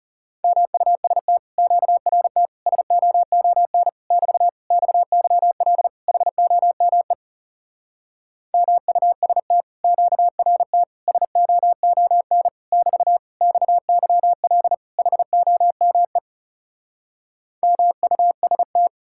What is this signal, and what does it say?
Morse code, heard as a keyed tone, about 40 wpm, 700 Hz. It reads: MUST QRT SOON <BT> XYL HOME MUST QRT SOON <BT> XYL HOME MUST